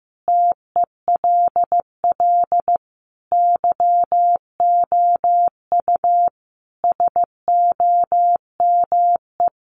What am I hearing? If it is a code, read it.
Morse code, heard as a keyed tone, about 15 wpm, 700 Hz. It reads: TELL YOU SOME
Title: Sentences from Top 800 Words 15wpm